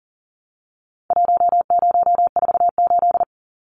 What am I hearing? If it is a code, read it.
Morse code, heard as a keyed tone, about 40 wpm, 700 Hz. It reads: 1048
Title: Numbers - Contest Style - 4-Digit 40wpm